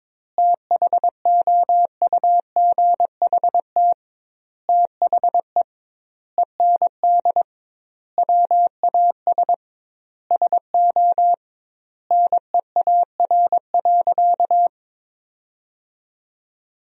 Morse code, heard as a keyed tone, about 22 wpm, 700 Hz. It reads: THOUGHT THE END WAS SO NEAR.
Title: 2x Word-Spacing - Sentences from Top 300 Words - 22wpm